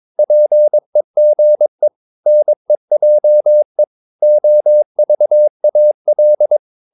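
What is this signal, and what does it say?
Morse code, heard as a keyed tone, about 22 wpm, 600 Hz. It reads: PEGE NEJE OVAL
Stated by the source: Morse practice of short Danish words at 22 wpm, from CW med Gnister.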